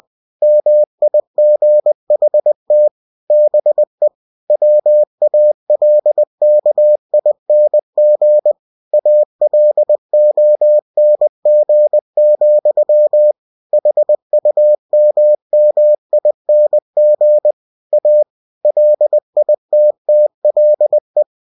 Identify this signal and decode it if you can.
Morse code, heard as a keyed tone, about 20 wpm, 600 Hz. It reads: MIGHT BE WALKING ALONG, HUMMING A LITTLE
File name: winnie-20wpm